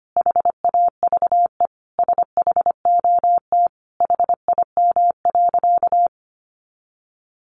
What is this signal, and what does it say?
Morse code, heard as a keyed tone, about 25 wpm, 700 Hz. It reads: HAVE SHOT HIM.
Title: Sentences from Top 1100 Words 25wpm